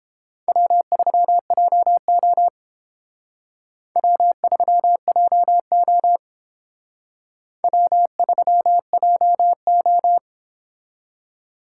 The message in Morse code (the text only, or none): W3JO W3JO W3JO